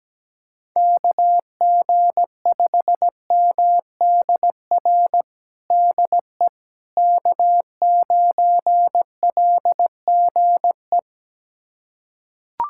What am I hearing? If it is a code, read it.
Morse code, heard as a keyed tone, about 17 wpm, 700 Hz. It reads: KG5MDR DE K9LGE
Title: USA Callsign DE Callsign 17wpm